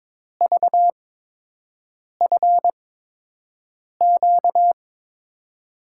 Morse code, heard as a keyed tone, about 22 wpm, spaced as for 6 wpm, 700 Hz. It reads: VFQ
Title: USA Call Signs - 5 Characters - 8x Character Spacing - 22wpm